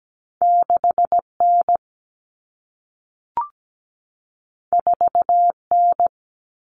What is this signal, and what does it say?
Morse code, heard as a keyed tone, about 17 wpm, 700 Hz. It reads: 6N 4N
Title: Common Cut Numbers - 2-Digit 17wpm